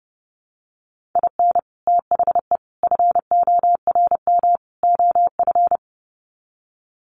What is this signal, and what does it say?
Morse code, heard as a keyed tone, about 30 wpm, 700 Hz. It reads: IN THE FORM OF
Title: Sets of 4 Words 30wpm